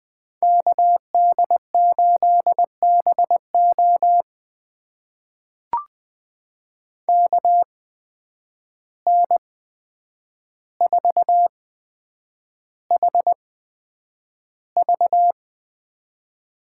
Morse code, heard as a keyed tone, about 20 wpm, spaced as for 9 wpm, 700 Hz. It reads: KD8BO KN4HV
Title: USA Call Signs - 8x Character Spacing - 20wpm